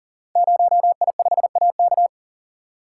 Morse code, heard as a keyed tone, about 43 wpm, 700 Hz. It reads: 0I5AK
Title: Code Groups - 5 Letter:Numbers - 40wpm